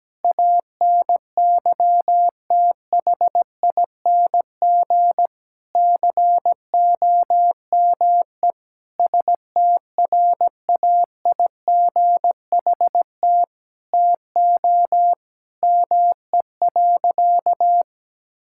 Morse code, heard as a keyed tone, about 17 wpm, 700 Hz. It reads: ANYTHING COME STRAIGHT TO ME.
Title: Sentences from Top 1000 Words - Review Part 6 - 17wpm